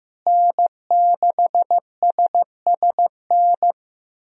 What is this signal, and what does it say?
Morse code, heard as a keyed tone, about 15 wpm, 700 Hz. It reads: N6SSN
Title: USA Call Signs - 5 Characters - Repeated 3x - 15wpm